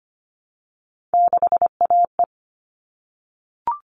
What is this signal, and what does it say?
Morse code, heard as a keyed tone, about 25 wpm, 700 Hz. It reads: 6AE
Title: All Cut Numbers - 3-Digit 25wpm